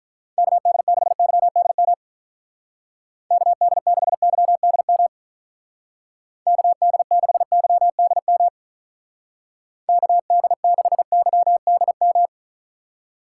Morse code, heard as a keyed tone, about 44 wpm, 700 Hz. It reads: KD6YDM KD6YDM KD6YDM KD6YDM